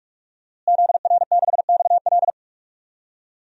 Morse code, heard as a keyed tone, about 46 wpm, 700 Hz. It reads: ZR6XL
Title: DXCC-100 Call Signs - 5 Characters - Repeated 2x - 45wpm